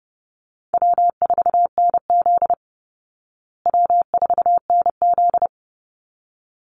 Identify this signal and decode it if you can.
Morse code, heard as a keyed tone, about 30 wpm, 700 Hz. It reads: W4NZ W4NZ
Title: Easy Call Signs Repeated 2x - 30wpm